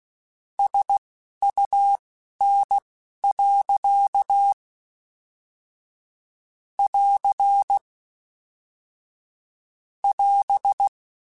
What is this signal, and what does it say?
Morse code, heard as a keyed tone, about 16 wpm, spaced as for 11 wpm, 800 Hz. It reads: SUN. <AR> <AS>
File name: Weather Sets / Weather Sets-mp3s / Weather 5 5 @10